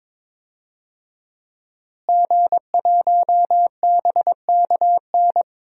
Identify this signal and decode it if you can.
Morse code, heard as a keyed tone, about 22 wpm, 700 Hz. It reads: G1BKN
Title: DXCC-100 Call Signs - 5 Characters - Repeated 3x - 22wpm